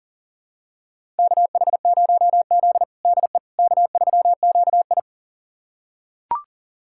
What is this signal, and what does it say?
Morse code, heard as a keyed tone, about 40 wpm, 690 Hz. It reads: KH0Z DE K3QI
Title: USA Callsign DE Callsign - 4 Characters - 40wpm